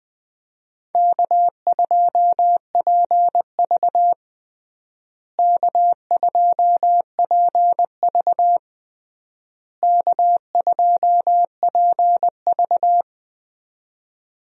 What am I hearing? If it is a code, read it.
Morse code, heard as a keyed tone, about 20 wpm, 700 Hz. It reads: K2PV K2PV K2PV
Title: USA Call Signs - 4 Characters - Repeated 3x - 20wpm